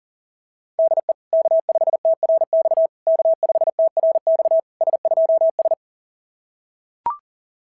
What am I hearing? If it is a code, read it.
Morse code, heard as a keyed tone, about 40 wpm, 690 Hz. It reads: DE K5TRX K5TRX S2S
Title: SOTA QSO Practice 40wpm